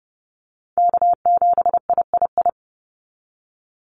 Morse code, heard as a keyed tone, about 30 wpm, 700 Hz. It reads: K7III
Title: NA SOTA Chaser Callsigns 30wpm